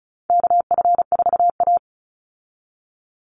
Morse code, heard as a keyed tone, about 34 wpm, 700 Hz. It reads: KF4A
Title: Call Signs Repeated 2x - 35wpm